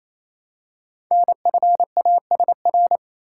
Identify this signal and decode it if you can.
Morse code, heard as a keyed tone, about 27 wpm, 700 Hz. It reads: NFASR
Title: Code Groups - 5 Letters - 28wpm